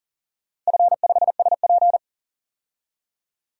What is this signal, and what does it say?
Morse code, heard as a keyed tone, about 40 wpm, 700 Hz. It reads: F5SP
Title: DXCC-100 Call Signs - 4 Characters - 40wpm